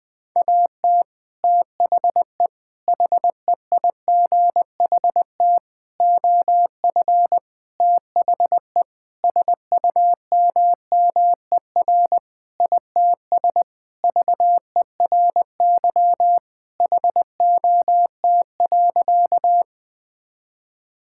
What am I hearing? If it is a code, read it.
Morse code, heard as a keyed tone, about 20 wpm, 700 Hz. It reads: AT THE HEIGHT OF THE SUMMER ITS VERY HOT.
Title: Sentences with 5-Word Phrases - Spoken Beginning - 20wpm